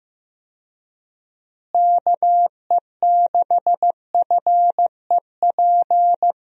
Morse code, heard as a keyed tone, about 15 wpm, 700 Hz. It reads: KE6FEP
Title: USA Call Signs - 6 Characters - Repeated 2x - 15wpm